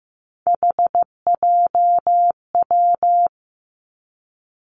Morse code, heard as a keyed tone, about 15 wpm, 700 Hz. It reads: HJW
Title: Three Challenging Letter-Number Combinations 15wpm